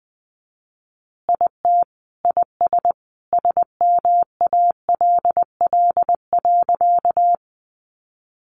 Morse code, heard as a keyed tone, about 20 wpm, 700 Hz. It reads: IT IS SMALL.
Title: Easy Phrases 20wpm